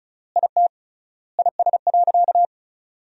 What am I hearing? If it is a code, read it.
Morse code, heard as a keyed tone, about 36 wpm, 700 Hz. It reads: IT IS.
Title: 3x Word-Spacing - Sentences from Top 400 Words - 35wpm